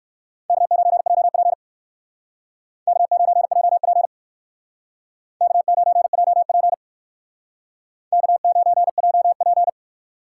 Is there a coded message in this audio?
Yes: Morse code from 0.5 to 9.7 s, heard as a keyed tone, about 64 wpm, 700 Hz. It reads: K9JP K9JP K9JP K9JP